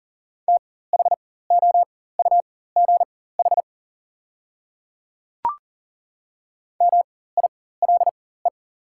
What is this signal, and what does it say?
Morse code, heard as a keyed tone, about 39 wpm, spaced as for 18 wpm, 690 Hz. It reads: THOUGH MILE
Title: 4x Character Spacing - Top 500 Words - 40wpm